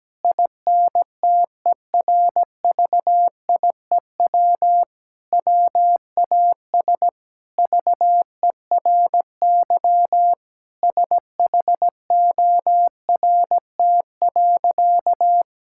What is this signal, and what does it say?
Morse code, heard as a keyed tone, about 17 wpm, 700 Hz. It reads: INTERVIEW WAS VERY SHORT.
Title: Sentences from Top 1600 Words 17wpm